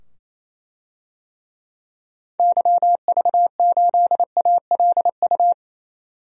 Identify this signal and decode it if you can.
Morse code, heard as a keyed tone, about 28 wpm, 700 Hz. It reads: YV8ALU
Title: DXCC-100 Call Signs - 6 Characters - 28wpm